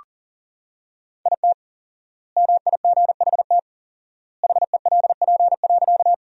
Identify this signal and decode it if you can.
Morse code, heard as a keyed tone, about 40 wpm, 700 Hz. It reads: IT MIGHT HELP.
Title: 4x Word-Spacing - Sentences from Top 300 Words - 40wpm